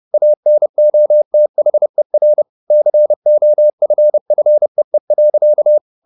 Morse code, heard as a keyed tone, about 30 wpm, 600 Hz. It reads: ANOTHER COFFEE.
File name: the old man and the sea / the old man and the sea 1 30wpm